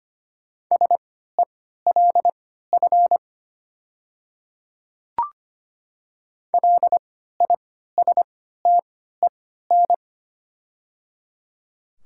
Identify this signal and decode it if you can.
Morse code, heard as a keyed tone, about 25 wpm, spaced as for 14 wpm, 700 Hz. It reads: SELF LISTEN
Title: 3x Character Spacing - Top 500 Words - 25wpm